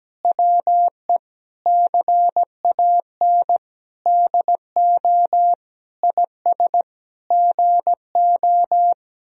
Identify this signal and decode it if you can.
Morse code, heard as a keyed tone, about 17 wpm, 700 Hz. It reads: WE CAN DO IS GO
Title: Sentences from Top 200 Words and Abbreviations 17wpm